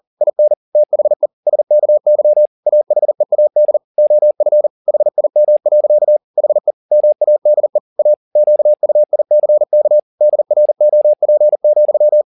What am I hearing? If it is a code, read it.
Morse code, heard as a keyed tone, about 40 wpm, 600 Hz. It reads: IN THE SKY AHEAD OF HIM. HE MADE A QUICK DROP,